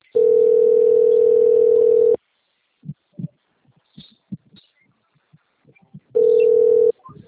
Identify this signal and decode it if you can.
Ringback tone (North America), 440 + 480 Hz.